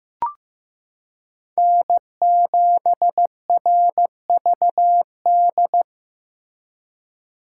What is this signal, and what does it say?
Morse code, heard as a keyed tone, about 15 wpm, 700 Hz. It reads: N7RVD
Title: K1USN SST QSO Practice - 15wpm